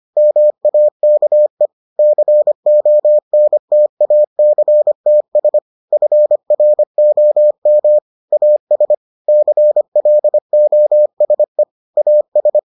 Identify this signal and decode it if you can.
Morse code, heard as a keyed tone, about 25 wpm, 600 Hz. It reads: MAKE CONTACTS FROM AS CLOSE AS